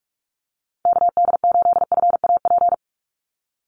Morse code, heard as a keyed tone, about 44 wpm, 700 Hz. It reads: KD8FAP
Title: USA Call Signs 45wpm